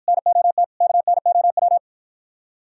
Morse code, heard as a keyed tone, about 52 wpm, 700 Hz. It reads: NOT KNOW